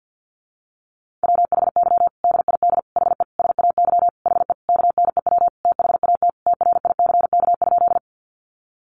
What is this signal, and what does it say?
Morse code, heard as a keyed tone, about 50 wpm, 700 Hz. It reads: WHY DID HE SAY HE KNEW THAT TRICK?
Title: Sentences from Top 1500 Words 50wpm